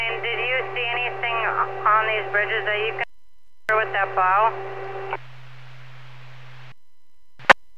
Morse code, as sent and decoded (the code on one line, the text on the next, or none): none